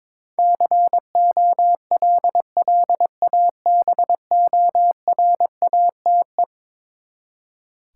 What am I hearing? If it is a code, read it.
Morse code, heard as a keyed tone, about 22 wpm, 700 Hz. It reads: COLLABORATE
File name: practice / 22wpm / Eleven Letter Words - 22wpm